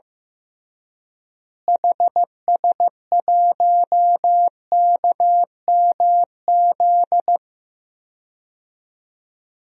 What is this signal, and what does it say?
Morse code, heard as a keyed tone, about 16 wpm, 700 Hz. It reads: HS1KMZ
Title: DXCC-100 Call Signs - 6 Characters - Repeated 2x - 15wpm